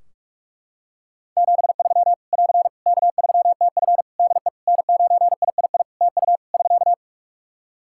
Morse code, heard as a keyed tone, about 42 wpm, 700 Hz. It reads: 73 <AR> K3TF DE N9III TU <SK>
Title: Practice QSO - Exchange 4 - Ending - 45wpm